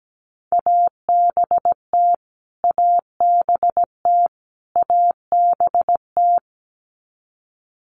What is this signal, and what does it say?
Morse code, heard as a keyed tone, about 17 wpm, 700 Hz. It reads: ABT ABT ABT
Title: Abbreviations Repeated 3x - 17wpm